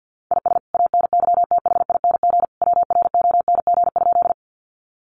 Morse code, heard as a keyed tone, about 50 wpm, 700 Hz. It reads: IS ANYTHING WRONG?
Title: Sentences from Top 1000 Words 50wpm